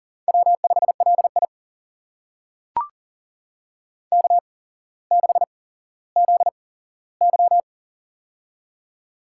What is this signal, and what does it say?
Morse code, heard as a keyed tone, about 39 wpm, spaced as for 17 wpm, 700 Hz. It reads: W5LI K6ZY
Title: USA Call Signs - 4 Characters - 8x Character Spacing - 40wpm